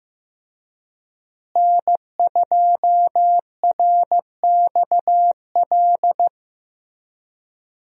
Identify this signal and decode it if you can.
Morse code, heard as a keyed tone, about 15 wpm, 700 Hz. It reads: N2RXL
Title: USA Call Signs - 5 Characters - Repeated 2x - 15wpm